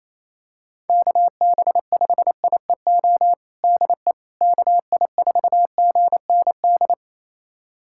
Morse code, heard as a keyed tone, about 28 wpm, 700 Hz. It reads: KB5IEO DE KI4GND
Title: USA Callsign DE Callsign 28wpm